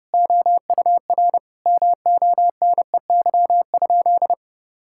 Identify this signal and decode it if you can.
Morse code, heard as a keyed tone, about 30 wpm, 700 Hz. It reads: OUR MONEY?
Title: Sentences from Top 1000 Words - Review Part 5 - 30wpm